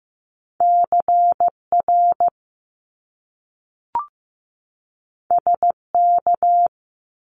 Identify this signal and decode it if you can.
Morse code, heard as a keyed tone, about 15 wpm, 700 Hz. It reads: CR SK
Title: Top Two Letter Combinations in Words 15wpm